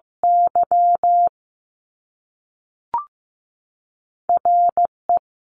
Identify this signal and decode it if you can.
Morse code, heard as a keyed tone, about 15 wpm, 700 Hz. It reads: Y RE
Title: Top Two Letter Combinations in Words 15wpm